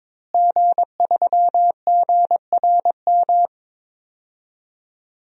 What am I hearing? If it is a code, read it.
Morse code, heard as a keyed tone, about 22 wpm, 700 Hz. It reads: G3GRM
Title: DXCC-100 Call Signs - 5 Characters - Repeated 3x - 22wpm